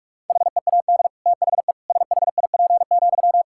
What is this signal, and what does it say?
Morse code, heard as a keyed tone, about 46 wpm, 700 Hz. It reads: HEAD THE SHIP,
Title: Sentences from Top 1000 Words - Review Part 5 - 45wpm